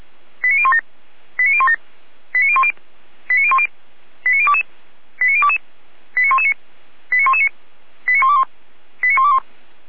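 Five-tone selective call (ZVEI), heard as quick runs of five tones, tones of about 80 ms. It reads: 89017 89017 89010 89010 89022 89022 89119 89119 89121 89121
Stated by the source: Five-tone ZVEI selective call 89014 received on SDR.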